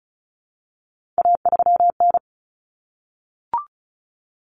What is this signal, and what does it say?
Morse code, heard as a keyed tone, about 34 wpm, 700 Hz. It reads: A3N
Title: Common Cut Numbers - 3-Digit 35wpm